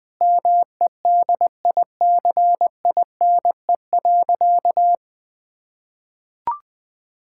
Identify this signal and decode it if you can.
Morse code, heard as a keyed tone, about 20 wpm, 700 Hz. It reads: MEDICINE.
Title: Sentences from Top 1000 Words - Review Part 6 - 20wpm